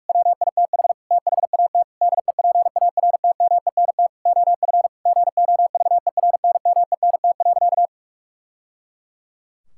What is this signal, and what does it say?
Morse code, heard as a keyed tone, about 46 wpm, 700 Hz. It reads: WITH THAT DEPARTMENT OF GOVERNMENT.
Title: Sentences from Top 1000 Words - Review Part 5 - 45wpm